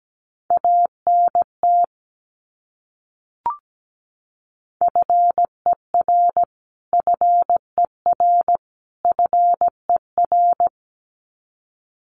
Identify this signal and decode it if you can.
Morse code, heard as a keyed tone, about 17 wpm, 700 Hz. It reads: ANT FER FER FER
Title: Abbreviations Repeated 3x - 17wpm